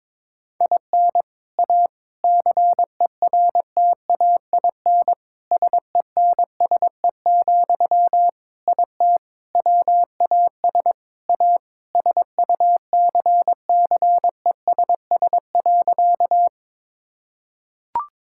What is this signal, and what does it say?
Morse code, heard as a keyed tone, about 21 wpm, 700 Hz. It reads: IN A CERTAIN SENSE, IT WAS A SUCCESS.
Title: Sentences from Top 1000 Words - Review Part 6 - 22wpm